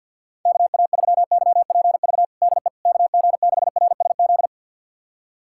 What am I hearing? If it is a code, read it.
Morse code, heard as a keyed tone, about 51 wpm, 700 Hz. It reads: KA3YPV DE KG6RSZ
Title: USA Callsign DE Callsign 50wpm